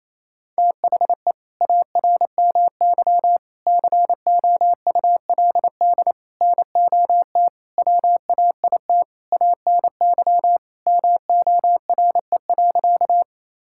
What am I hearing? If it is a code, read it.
Morse code, heard as a keyed tone, about 27 wpm, 700 Hz. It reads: THE ARMY COULD NOT WAIT ANY MORE.
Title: Sentences from Top 1000 Words - Review Part 5 - 28wpm